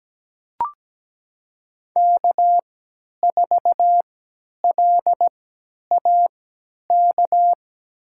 Morse code, heard as a keyed tone, about 17 wpm, spaced as for 10 wpm, 700 Hz. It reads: K4LAK
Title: USA Call Signs - 5 Characters - 3x Character Spacing - 17wpm